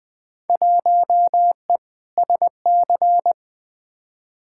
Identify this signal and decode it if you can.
Morse code, heard as a keyed tone, about 20 wpm, 700 Hz. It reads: 1E SC
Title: Field Day Exchanges - 20wpm